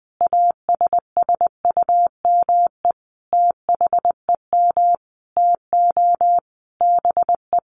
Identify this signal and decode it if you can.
Morse code, heard as a keyed tone, about 20 wpm, 700 Hz. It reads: ASSUME THEM TO BE
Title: Sentences from Top 1500 Words 20wpm